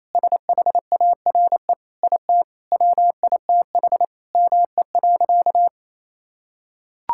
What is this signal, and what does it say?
Morse code, heard as a keyed tone, about 28 wpm, 700 Hz. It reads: SHARE IT WITH ME.
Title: Sentences from Top 1000 Words - Review Part 6 - 28wpm